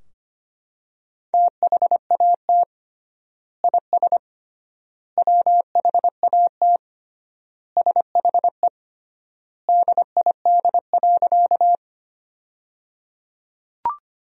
Morse code, heard as a keyed tone, about 25 wpm, 700 Hz. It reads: THAT IS WHAT SHE DID.
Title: 3x Word-Spacing - Sentences from Top 100 Words - 25wpm